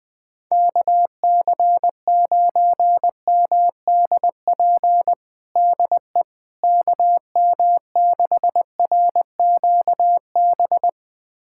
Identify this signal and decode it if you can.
Morse code, heard as a keyed tone, about 20 wpm, 700 Hz. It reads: KC9MDP DE KM6RQB
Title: USA Callsign DE Callsign - 6 Characters - 20wpm